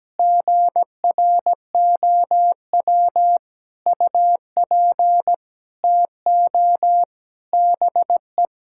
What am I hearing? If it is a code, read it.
Morse code, heard as a keyed tone, about 17 wpm, 700 Hz. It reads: GROW UP TO BE
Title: Sentences from Top 1000 Words - Review Part 6 - 17wpm